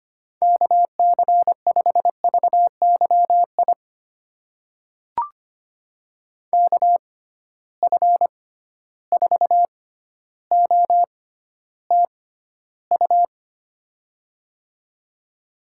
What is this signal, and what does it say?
Morse code, heard as a keyed tone, about 25 wpm, spaced as for 13 wpm, 700 Hz. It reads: KC5VYI KF4OTU
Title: USA Call Signs - 6 Characters - 6x Character Spacing - 25wpm